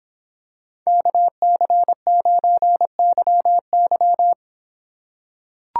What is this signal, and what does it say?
Morse code, heard as a keyed tone, about 26 wpm, 700 Hz. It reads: KC9YY